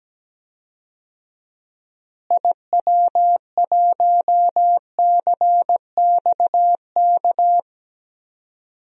Morse code, heard as a keyed tone, about 17 wpm, 700 Hz. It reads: IW1CXK
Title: DXCC-100 Call Signs - 6 Characters - Repeated 2x - 17wpm